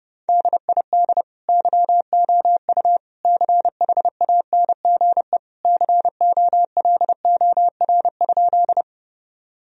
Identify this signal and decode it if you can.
Morse code, heard as a keyed tone, about 32 wpm, 700 Hz. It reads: DID YOU CHANGE COLOR?
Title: Sentences from Top 1000 Words - Review Part 2 - 30wpm